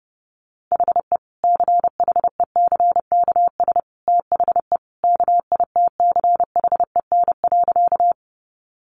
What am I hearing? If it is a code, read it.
Morse code, heard as a keyed tone, about 30 wpm, 700 Hz. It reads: HE CHECKS THE KITCHEN.
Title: Easy Phrases 30wpm